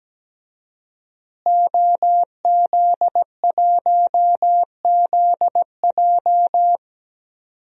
Morse code, heard as a keyed tone, about 17 wpm, 700 Hz. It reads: OZ1ZJ
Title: DXCC-100 Call Signs - 5 Characters - Repeated 3x - 17wpm